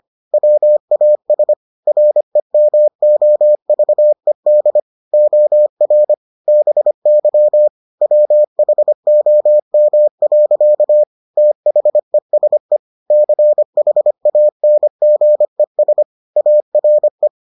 Morse code, heard as a keyed tone, about 25 wpm, 600 Hz. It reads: WAS REMOVED OR BY WHOM. THESE CHANGES ARE